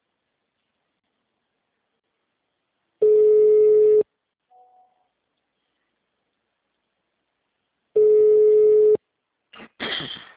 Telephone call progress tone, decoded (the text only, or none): ringback tone